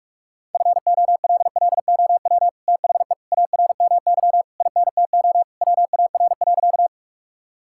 Morse code, heard as a keyed tone, about 47 wpm, 700 Hz. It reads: FOLLOW THE ARMY INTO WAR.
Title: Sentences from Top 1000 Words - Review Part 5 - 45wpm